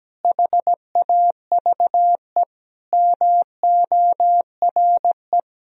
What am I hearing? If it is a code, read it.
Morse code, heard as a keyed tone, about 17 wpm, 700 Hz. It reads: HAVE MORE
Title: Sentences from Top 1000 Words - Review Part 3 - 17wpm